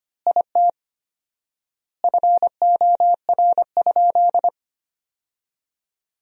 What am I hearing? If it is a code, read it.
Morse code, heard as a keyed tone, about 25 wpm, 700 Hz. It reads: IT FOR?
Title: 4x Word-Spacing - Sentences from Top 100 Words - 25wpm